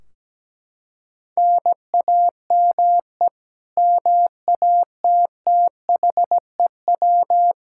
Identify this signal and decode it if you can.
Morse code, heard as a keyed tone, about 17 wpm, 700 Hz. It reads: NAME MATTHEW
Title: K1USN SST QSO Practice 17wpm